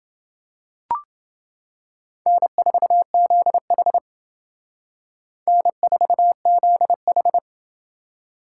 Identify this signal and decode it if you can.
Morse code, heard as a keyed tone, about 28 wpm, 700 Hz. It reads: N4ZH N4ZH